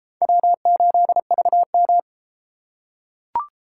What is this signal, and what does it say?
Morse code, heard as a keyed tone, about 32 wpm, 700 Hz. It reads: W8VM